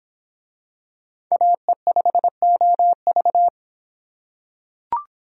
Morse code, heard as a keyed tone, about 26 wpm, 700 Hz. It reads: AE5OV